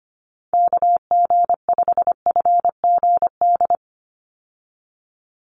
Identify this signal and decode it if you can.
Morse code, heard as a keyed tone, about 25 wpm, 700 Hz. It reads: KG5FGD
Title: USA Call Signs 25wpm